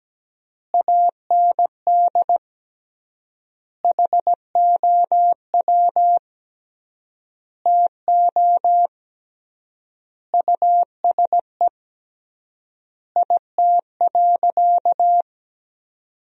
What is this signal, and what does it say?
Morse code, heard as a keyed tone, about 17 wpm, 700 Hz. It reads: AND HOW TO USE IT.
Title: 3x Word-Spacing - Sentences from Top 500 Words - 17wpm